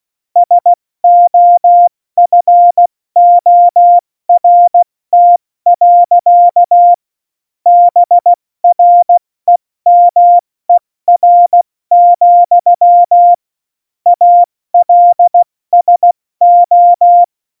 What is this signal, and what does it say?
Morse code, heard as a keyed tone, about 16 wpm, 700 Hz. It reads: SOFORT. BREMER, ALSO
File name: Morsekurs / Witze / Witz 05. Januar 2021, 70 F80